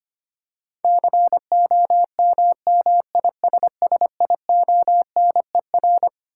Morse code, heard as a keyed tone, about 25 wpm, 700 Hz. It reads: COMMISSIONER
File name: Twelve Letter Words - 25wpm